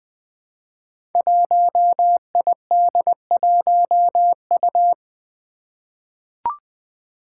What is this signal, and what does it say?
Morse code, heard as a keyed tone, about 20 wpm, 700 Hz. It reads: 1ID1U
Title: Five Challenging Letter-Number Combinations 20wpm